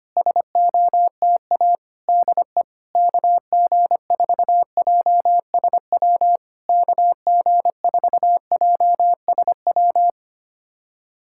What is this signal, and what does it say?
Morse code, heard as a keyed tone, about 25 wpm, 700 Hz. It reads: SOTA DE KG4JSW KG4JSW
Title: SOTA QSO Practice 25wpm